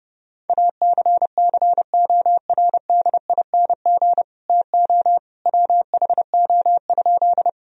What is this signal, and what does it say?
Morse code, heard as a keyed tone, about 30 wpm, 700 Hz. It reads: ACCORDING TO WHO?
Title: Sentences from Top 1000 Words - Review Part 4 - 30wpm